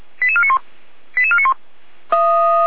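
Five-tone selective call (ZVEI), heard as quick runs of five tones, tones of about 70 ms. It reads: 88491 88491
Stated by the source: ZVEI five-tone call 88485 received on SDR.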